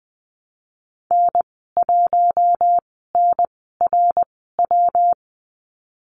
Morse code, heard as a keyed tone, about 19 wpm, spaced as for 14 wpm, 700 Hz. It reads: N1NRW
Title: 128 Lesson 21g - Number 6 - Call Signs 20wpm